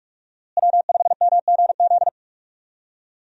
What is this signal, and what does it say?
Morse code, heard as a keyed tone, about 46 wpm, 700 Hz. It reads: W5MGZ
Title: USA Call Signs - 5 Characters - Repeated 2x - 45wpm